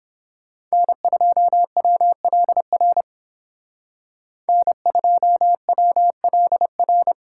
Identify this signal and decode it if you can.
Morse code, heard as a keyed tone, about 28 wpm, 700 Hz. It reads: N2WLR N2WLR